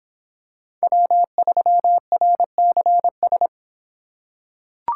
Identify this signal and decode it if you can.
Morse code, heard as a keyed tone, about 25 wpm, 700 Hz. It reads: W3RCS